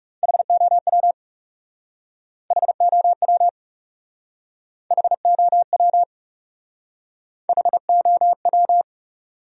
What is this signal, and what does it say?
Morse code, heard as a keyed tone, about 39 wpm, 700 Hz. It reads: HOW HOW HOW HOW